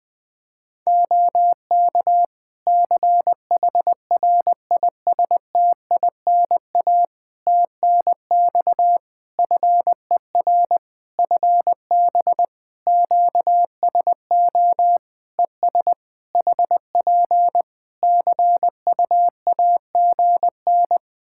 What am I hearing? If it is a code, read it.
Morse code, heard as a keyed tone, about 20 wpm, 700 Hz. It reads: OK CHRISTINA TNX FER FB QSO ES HP CUAGN
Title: Practice QSO - Exchange 4 - Ending - 20wpm